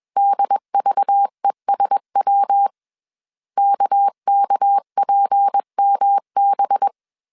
Morse code, heard as a keyed tone, about 21 wpm, 800 Hz. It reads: DVESW KKPMB
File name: CW / DET / 1 6 1500C S B 20wpm 2048